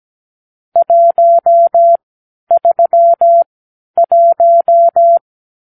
Morse code, heard as a keyed tone, about 17 wpm, spaced as for 11 wpm, 680 Hz. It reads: 131
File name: k7qo / Lesson 44 (1-3)